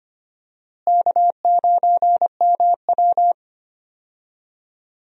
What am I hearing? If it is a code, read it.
Morse code, heard as a keyed tone, about 25 wpm, 700 Hz. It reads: K9MW
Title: USA Call Signs - 4 Characters - Repeated 2x - 25wpm